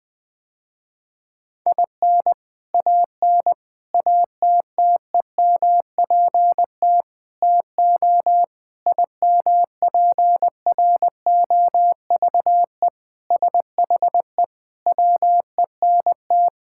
Morse code, heard as a keyed tone, about 20 wpm, 700 Hz. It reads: IN AN ATTEMPT TO IMPROVE SHE WENT
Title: Sentences with 4-Word Phrases - Spoken Beginning - 20wpm